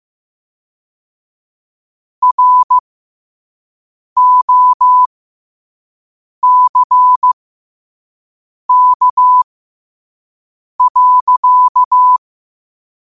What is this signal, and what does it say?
Morse code, heard as a keyed tone, about 15 wpm, spaced as for 5 wpm, 1000 Hz. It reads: ROCK.